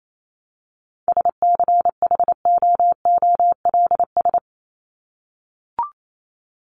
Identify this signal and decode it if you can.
Morse code, heard as a keyed tone, about 28 wpm, 700 Hz. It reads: SCHOOLS
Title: ES:S Suffix - Top 100 Words with Base Words 28wpm